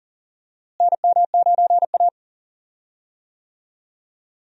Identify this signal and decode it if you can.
Morse code, heard as a keyed tone, about 40 wpm, 700 Hz. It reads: NM9A
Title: USA Call Signs - 4 Characters - Repeated 3x - 40wpm